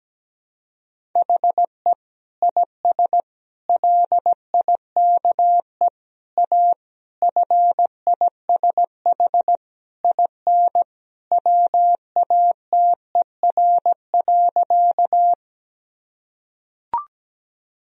Morse code, heard as a keyed tone, about 17 wpm, 700 Hz. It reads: HE IS LIKE A FISH IN WATER.
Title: Sentences from Top 1000 Words - Review Part 1 - 17wpm